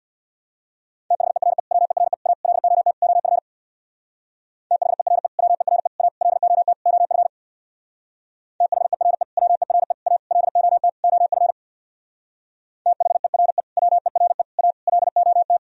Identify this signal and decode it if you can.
Morse code, heard as a keyed tone, about 64 wpm, 700 Hz. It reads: THERE WERE A LOT OF THERE WERE A LOT OF THERE WERE A LOT OF THERE WERE A LOT